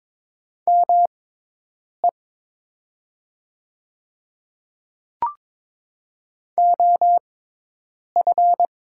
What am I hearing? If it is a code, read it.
Morse code, heard as a keyed tone, about 22 wpm, spaced as for 8 wpm, 700 Hz. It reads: ME OF
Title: 6x Character Spacing - Top 300 Words - 22wpm